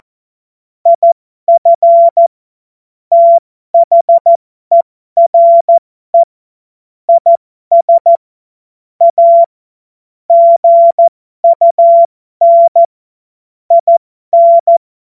Morse code, heard as a keyed tone, about 14 wpm, spaced as for 12 wpm, 680 Hz. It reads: IF THERE IS A GUN IN